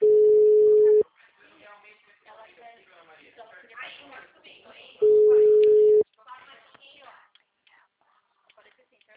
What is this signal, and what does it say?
Ringback tone (Europe, 425 Hz), with speech.